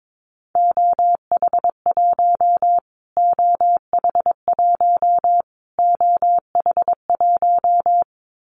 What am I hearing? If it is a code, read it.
Morse code, heard as a keyed tone, about 22 wpm, 700 Hz. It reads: OH1 OH1 OH1
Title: Callsign Prefixes for Top 25 DXCC - Repeated 3x - 22wpm